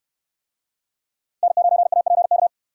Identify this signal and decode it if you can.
Morse code, heard as a keyed tone, about 69 wpm, 700 Hz. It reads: N9APK